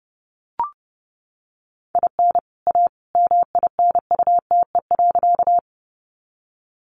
Morse code, heard as a keyed tone, about 30 wpm, 700 Hz. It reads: IN A MINUTE.
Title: Sentences from Top 500 Words 30wpm